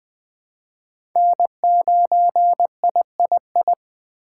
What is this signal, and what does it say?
Morse code, heard as a keyed tone, about 20 wpm, 700 Hz. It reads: N9III
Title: USA Call Signs - 5 Characters - 3x Character Spacing - 20wpm